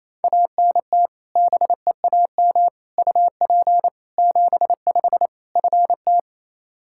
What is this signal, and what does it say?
Morse code, heard as a keyed tone, about 28 wpm, 700 Hz. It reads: ANT BEAM UP 75 FT
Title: Practice QSO - Exchange 2 - Info - 28wpm